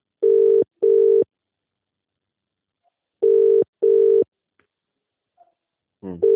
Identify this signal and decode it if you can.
Ringback tone (UK), 400 + 450 Hz.